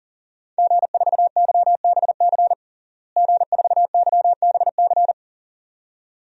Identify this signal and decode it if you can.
Morse code, heard as a keyed tone, about 40 wpm, 700 Hz. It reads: G4YBC G4YBC
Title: DXCC-100 Call Signs - 5 Characters - Repeated 2x - 40wpm